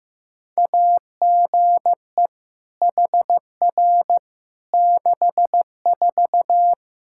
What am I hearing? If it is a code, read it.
Morse code, heard as a keyed tone, about 15 wpm, 700 Hz. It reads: AGE HR 64
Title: K1USN SST QSO Practice 15wpm